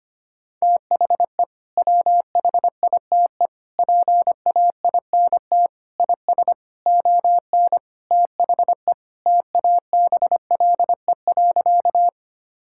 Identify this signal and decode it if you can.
Morse code, heard as a keyed tone, about 25 wpm, 700 Hz. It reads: THE WHITE PAINT IS ON THE TABLE.
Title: Sentences from Top 1000 Words - Review Part 6 - 25wpm